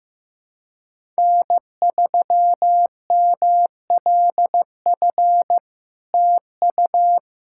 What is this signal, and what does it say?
Morse code, heard as a keyed tone, about 15 wpm, 700 Hz. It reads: N3MLF TU
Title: SOTA QSO Practice 15wpm